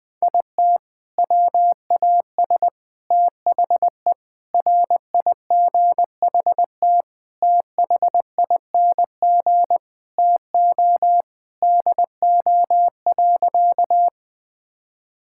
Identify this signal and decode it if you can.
Morse code, heard as a keyed tone, about 20 wpm, 700 Hz. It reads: IT WAS THE RIGHT THING TO DO.
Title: Sentences with 6-Word Phrases - 20wpm